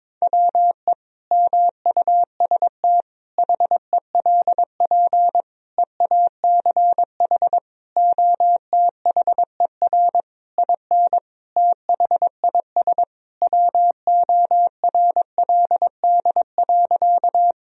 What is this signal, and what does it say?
Morse code, heard as a keyed tone, about 22 wpm, 700 Hz. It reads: WE MUST HELP EACH OTHER IN THIS WORLD.
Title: Sentences from Top 1000 Words - Review Part 1 - 22wpm